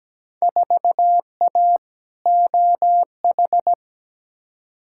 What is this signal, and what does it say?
Morse code, heard as a keyed tone, about 17 wpm, 700 Hz. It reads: 4A OH
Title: Field Day Exchanges - 17wpm